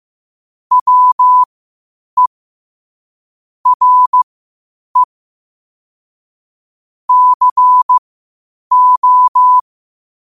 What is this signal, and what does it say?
Morse code, heard as a keyed tone, about 15 wpm, spaced as for 8 wpm, 1000 Hz. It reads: WE RE CO
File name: qotdfiles / 8wpm